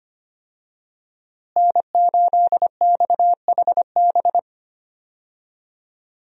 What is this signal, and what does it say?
Morse code, heard as a keyed tone, about 25 wpm, 700 Hz. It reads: N8XHB
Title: USA Call Signs - 5 Characters - Repeated 2x - 25wpm